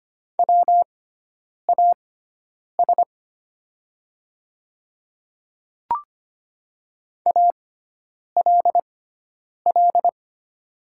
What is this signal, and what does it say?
Morse code, heard as a keyed tone, about 25 wpm, spaced as for 9 wpm, 700 Hz. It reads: WAS ALL
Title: 6x Character Spacing - Top 100 Words - 25wpm